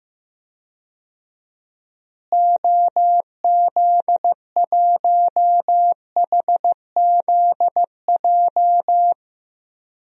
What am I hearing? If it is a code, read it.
Morse code, heard as a keyed tone, about 15 wpm, 700 Hz. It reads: OZ1HZJ
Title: DXCC-100 Call Signs - 6 Characters - Repeated 3x - 15wpm